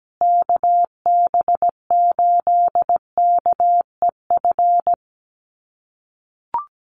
Morse code, heard as a keyed tone, about 17 wpm, 700 Hz. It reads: KB8KEF
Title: USA Call Signs Repeated 3x 17wpm